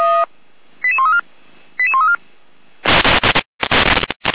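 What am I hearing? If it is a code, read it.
Five-tone selective call (ZVEI), heard as quick runs of five tones, tones of about 70 ms. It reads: 88134 88134
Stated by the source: ZVEI five-tone selective call received by SDR.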